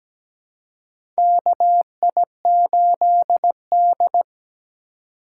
Morse code, heard as a keyed tone, about 17 wpm, 700 Hz. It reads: KI8D
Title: USA Call Signs - 4 Characters - 2x Character Spacing - 17wpm